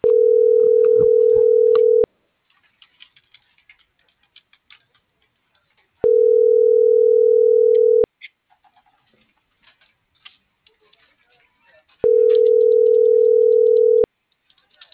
Ringback tone (North America), 440 + 480 Hz.